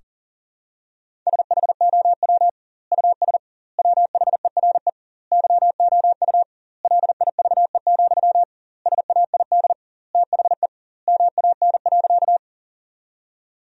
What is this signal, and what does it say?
Morse code, heard as a keyed tone, about 40 wpm, 700 Hz. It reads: SHOW US WHERE YOU LIVE, SAID THE MAN.
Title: 2x Word-Spacing - Sentences from Top 200 Words - 40wpm